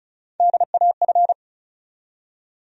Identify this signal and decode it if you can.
Morse code, heard as a keyed tone, about 36 wpm, 700 Hz. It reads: DAF
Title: USA Call Sign Postfixes - 3 Letters 35wpm